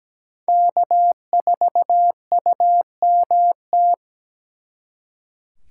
Morse code, heard as a keyed tone, about 20 wpm, 700 Hz. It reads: K4UMT
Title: Speed Racing - USA 1x3 Call Signs 17wpm